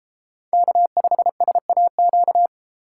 Morse code, heard as a keyed tone, about 31 wpm, 700 Hz. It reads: K5SAQ